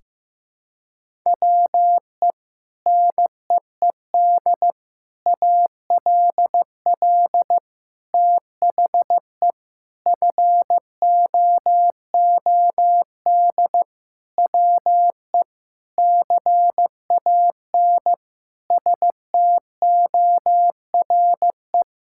Morse code, heard as a keyed tone, about 15 wpm, 700 Hz. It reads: WE NEED ALL THE FOOD WE CAN STORE
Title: Sentences from Top 1000 Words - Review Part 5 - 15wpm